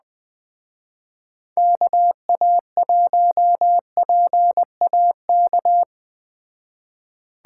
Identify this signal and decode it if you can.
Morse code, heard as a keyed tone, about 21 wpm, 700 Hz. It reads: KA1PAK